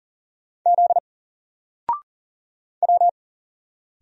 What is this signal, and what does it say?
Morse code, heard as a keyed tone, about 40 wpm, 700 Hz. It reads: Z W
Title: Single Letter-Number Alt 40wpm